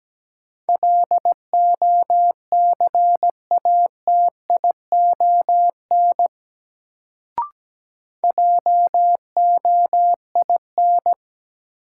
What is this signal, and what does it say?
Morse code, heard as a keyed tone, about 17 wpm, 700 Hz. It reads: LOCATION JOIN
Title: Sentences from Top 1000 Words - Review Part 6 - New Word Drill - 17wpm